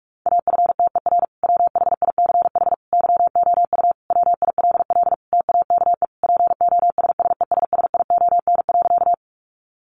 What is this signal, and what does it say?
Morse code, heard as a keyed tone, about 44 wpm, 700 Hz. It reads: AFTER WHICH YOU WILL TAKE POSSESSION.
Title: Sentences from Top 1400 Words 45wpm